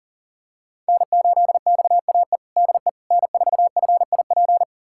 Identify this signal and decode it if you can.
Morse code, heard as a keyed tone, about 40 wpm, 700 Hz. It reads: N8XAE DE N4FIP
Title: USA Callsign DE Callsign - 5 Characters - 40wpm